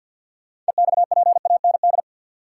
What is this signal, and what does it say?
Morse code, heard as a keyed tone, about 53 wpm, 700 Hz. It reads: EXPAND
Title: Sentences from Top 1400 Words - New Word Drill - 50wpm